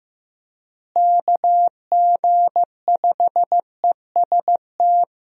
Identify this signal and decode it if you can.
Morse code, heard as a keyed tone, about 16 wpm, 700 Hz. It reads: KG5EST
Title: USA Call Signs - 6 Characters - Repeated 3x - 15wpm